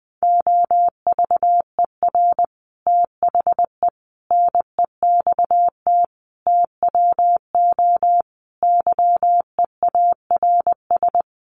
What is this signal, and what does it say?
Morse code, heard as a keyed tone, about 20 wpm, 700 Hz. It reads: OVER THE NEXT TWO YEARS
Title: Sets of 5 Words Encore2 20wpm